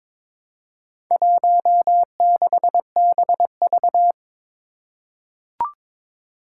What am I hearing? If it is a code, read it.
Morse code, heard as a keyed tone, about 23 wpm, 700 Hz. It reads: 16BV
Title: Four Challenging Letter-Number Combinations 22wpm